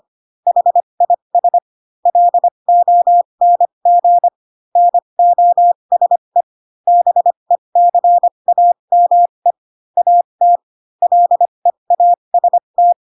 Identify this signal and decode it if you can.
Morse code, heard as a keyed tone, about 25 wpm, spaced as for 22 wpm, 700 Hz. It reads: HIS LONG NOSE BECAME AT LEAST